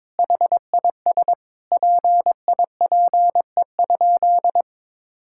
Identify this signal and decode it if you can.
Morse code, heard as a keyed tone, about 22 wpm, 700 Hz. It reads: HIS PIPE?
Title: Sentences from Top 1600 Words 22wpm